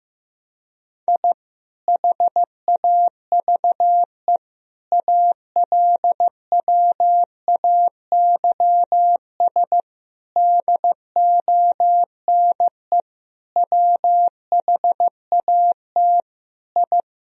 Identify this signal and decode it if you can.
Morse code, heard as a keyed tone, about 16 wpm, 700 Hz. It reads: I HAVE ALWAYS DONE WHAT I
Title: Sentences from Top 1000 Words - Review Part 2 - 15wpm